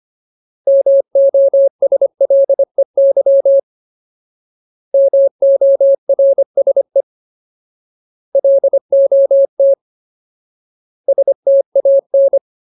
Morse code, heard as a keyed tone, about 25 wpm, 550 Hz. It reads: MOSLEY MORSE LOT STAN
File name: Ultimate Word List / 25WPM Ultimate